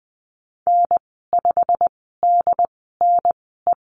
Morse code, heard as a keyed tone, about 20 wpm, spaced as for 15 wpm, 700 Hz. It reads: N5DNE
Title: 186 Lesson 30g - Letter b - Call Signs 20wpm